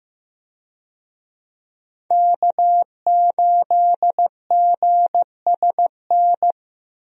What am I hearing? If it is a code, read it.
Morse code, heard as a keyed tone, about 15 wpm, 700 Hz. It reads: K8GSN
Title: USA Call Signs - 5 Characters - 15wpm